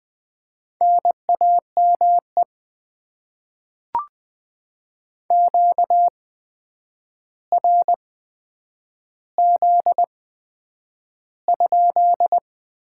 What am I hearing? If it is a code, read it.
Morse code, heard as a keyed tone, about 20 wpm, spaced as for 9 wpm, 700 Hz. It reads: NAME QRZ?
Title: Top 90 QSO Elements - 8x Character Spacing - 20wpm